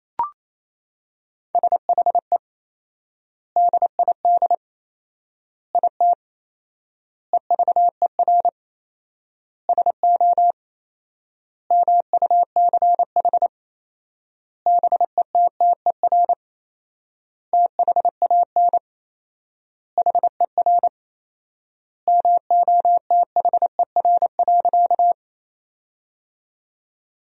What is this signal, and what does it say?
Morse code, heard as a keyed tone, about 28 wpm, 700 Hz. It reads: SHE DID IT EVER SO MUCH BETTER THAN HER MOTHER.
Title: 4x Word-Spacing - Sentences from Top 400 Words - 28wpm